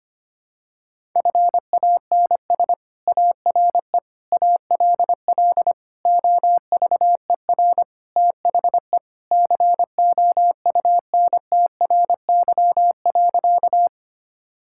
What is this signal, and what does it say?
Morse code, heard as a keyed tone, about 25 wpm, 700 Hz. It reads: FANS ARE ALL OVER THE COUNTRY.
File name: practice / 25wpm / Sentences with 4-Word Phrases - 25wpm